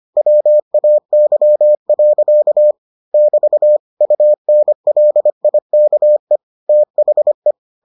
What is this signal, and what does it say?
Morse code, heard as a keyed tone, about 25 wpm, 600 Hz. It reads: WAY. <BT> UNLIKE THE